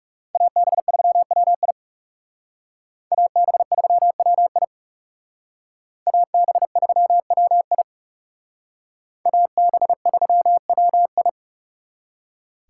Morse code, heard as a keyed tone, about 36 wpm, 700 Hz. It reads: AB3WI AB3WI AB3WI AB3WI
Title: Speed Racing - USA 2x2 Call Signs 30wpm